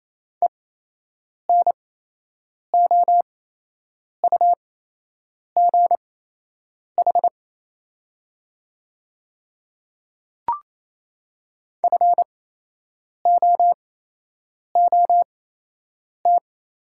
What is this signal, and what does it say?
Morse code, heard as a keyed tone, about 28 wpm, spaced as for 8 wpm, 700 Hz. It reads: ENOUGH FOOT
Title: 8x Character Spacing - Top 500 Words - 28wpm